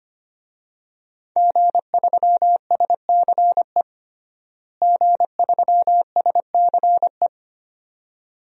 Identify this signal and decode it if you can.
Morse code, heard as a keyed tone, about 25 wpm, 700 Hz. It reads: G3SCE G3SCE
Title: DXCC-100 Call Signs - 5 Characters - Repeated 2x - 25wpm